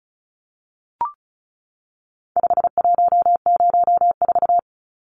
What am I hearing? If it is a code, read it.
Morse code, heard as a keyed tone, about 34 wpm, 700 Hz. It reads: 5104
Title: Numbers - 4-Digit 35wpm